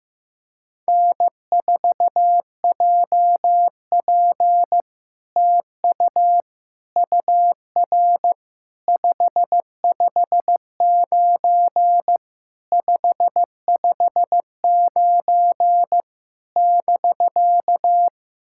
Morse code, heard as a keyed tone, about 15 wpm, 700 Hz. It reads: N4JP TU UR 559 559 <BK>
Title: SOTA QSO Practice 15wpm